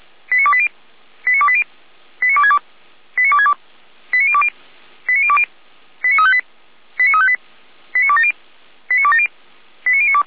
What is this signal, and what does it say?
Five-tone selective call (ZVEI), heard as quick runs of five tones, tones of about 70 ms. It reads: 89280 89280 89262 89262 89020 89020 89358 89358 89270 89270 89092
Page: ZVEI